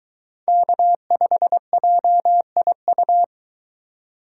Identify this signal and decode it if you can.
Morse code, heard as a keyed tone, about 22 wpm, 700 Hz. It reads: K5JIU